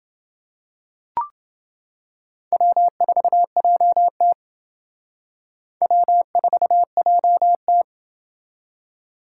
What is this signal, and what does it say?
Morse code, heard as a keyed tone, about 29 wpm, 700 Hz. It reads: W4JT W4JT